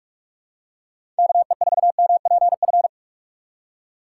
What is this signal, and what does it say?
Morse code, heard as a keyed tone, about 46 wpm, 700 Hz. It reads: KE4MPF
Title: USA Call Signs - 6 Characters - 6x Character Spacing - 45wpm